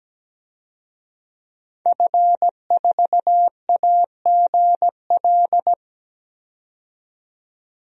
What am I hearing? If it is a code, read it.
Morse code, heard as a keyed tone, about 17 wpm, 700 Hz. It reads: F4AGL
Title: DXCC-100 Call Signs - 5 Characters - 17wpm